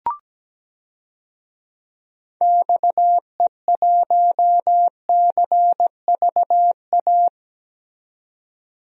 Morse code, heard as a keyed tone, about 17 wpm, 700 Hz. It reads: XE1CVA
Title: DXCC-100 Call Signs - 6 Characters - 17wpm